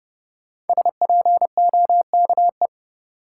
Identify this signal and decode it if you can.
Morse code, heard as a keyed tone, about 30 wpm, 700 Hz. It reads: SPOKE
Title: Sentences from Top 1200 Words - New Word Drill - 30wpm